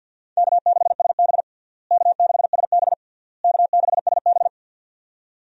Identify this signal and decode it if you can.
Morse code, heard as a keyed tone, about 51 wpm, 700 Hz. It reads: K6SB K6SB K6SB
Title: USA Call Signs - 4 Characters - Repeated 3x - 50wpm